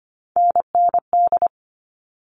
Morse code, heard as a keyed tone, about 25 wpm, 700 Hz. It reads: NND
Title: All Cut Numbers - 3-Digit 25wpm